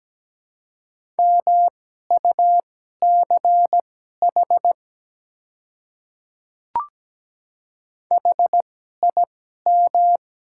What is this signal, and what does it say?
Morse code, heard as a keyed tone, about 17 wpm, spaced as for 12 wpm, 700 Hz. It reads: MUCH HIM
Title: 2x Character Spacing - Top 200 Words - 17wpm